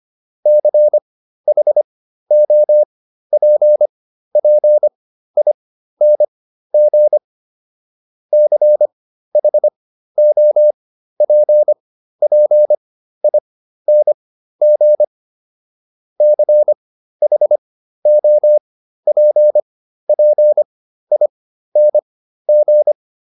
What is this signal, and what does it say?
Morse code, heard as a keyed tone, about 25 wpm, spaced as for 13 wpm, 600 Hz. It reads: CHOPPING CHOPPING CHOPPING
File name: ING8 13